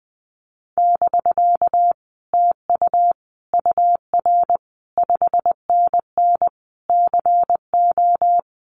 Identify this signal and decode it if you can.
Morse code, heard as a keyed tone, about 20 wpm, 700 Hz. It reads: <BK> TU UR 5NN CO
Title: Common QSO Phrases Encore 20wpm